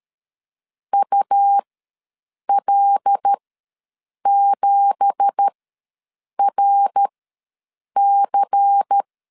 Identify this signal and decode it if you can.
Morse code, heard as a keyed tone, about 13 wpm, spaced as for 7 wpm, 800 Hz. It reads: UL7RC
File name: LEZIONI / 9 2867 ws 12wpm 3431